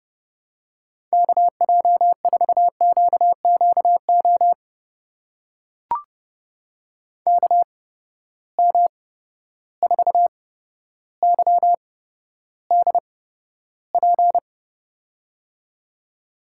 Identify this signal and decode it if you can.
Morse code, heard as a keyed tone, about 30 wpm, spaced as for 13 wpm, 700 Hz. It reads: KJ4QQO KM4YDP
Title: USA Call Signs - 6 Characters - 8x Character Spacing - 30wpm